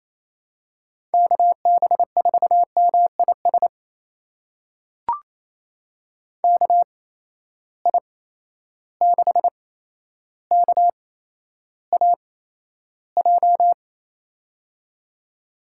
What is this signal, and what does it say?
Morse code, heard as a keyed tone, about 28 wpm, spaced as for 12 wpm, 700 Hz. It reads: KB4MIS KI6KAJ
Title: USA Call Signs - 6 Characters - 8x Character Spacing - 28wpm